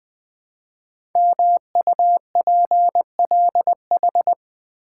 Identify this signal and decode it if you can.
Morse code, heard as a keyed tone, about 20 wpm, 700 Hz. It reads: MUPLH
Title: Code Groups - 5 Letters - 20wpm